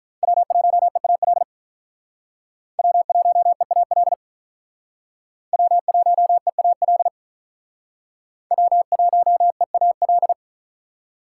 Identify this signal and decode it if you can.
Morse code, heard as a keyed tone, about 43 wpm, 700 Hz. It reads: W1EAL W1EAL W1EAL W1EAL
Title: Speed Racing - USA Call Signs 35wpm